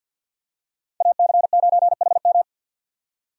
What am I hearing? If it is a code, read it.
Morse code, heard as a keyed tone, about 53 wpm, 700 Hz. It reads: AC9HM
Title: USA Call Signs - 5 Characters - 50wpm